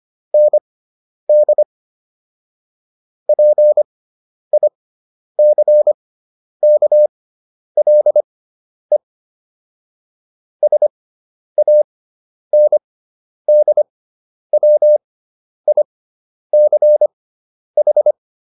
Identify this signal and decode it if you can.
Morse code, heard as a keyed tone, about 25 wpm, spaced as for 10 wpm, 600 Hz. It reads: ND PICKLE SANDWICH